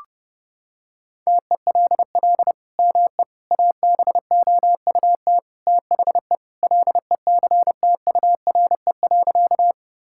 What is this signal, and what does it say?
Morse code, heard as a keyed tone, about 32 wpm, 700 Hz. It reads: TELL ME ABOUT THE LECTURE.
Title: Sentences from Top 1600 Words 30wpm